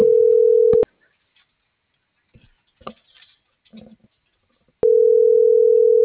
Ringback tone (North America), 440 + 480 Hz.